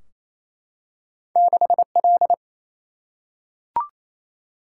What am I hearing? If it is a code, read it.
Morse code, heard as a keyed tone, about 28 wpm, 700 Hz. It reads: 6L
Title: Single-Letter Single-Number 28wpm